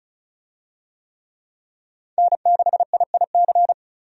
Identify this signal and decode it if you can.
Morse code, heard as a keyed tone, about 36 wpm, 700 Hz. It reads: N6IIC
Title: USA Call Signs - 5 Characters - Repeated 3x - 35wpm